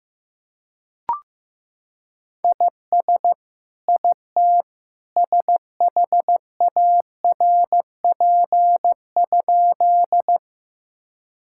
Morse code, heard as a keyed tone, about 15 wpm, 700 Hz. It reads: IS IT SHARP?
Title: Sentences from Top 1600 Words 15wpm